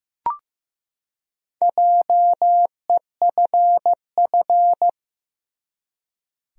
Morse code, heard as a keyed tone, about 15 wpm, 700 Hz. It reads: JEFF
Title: USA Common Names - Max 4 Letters - 15wpm